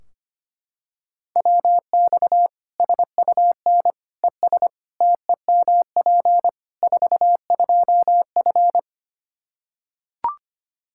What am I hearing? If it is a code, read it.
Morse code, heard as a keyed tone, about 25 wpm, 700 Hz. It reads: WX SUN ES TEMP 42F
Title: K1USN SST QSO Practice 25wpm